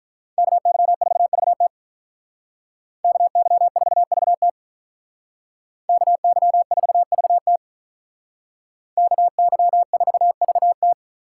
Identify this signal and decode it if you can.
Morse code, heard as a keyed tone, about 43 wpm, 700 Hz. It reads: KY4VT KY4VT KY4VT KY4VT
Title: Speed Racing - USA 2x2 Call Signs 35wpm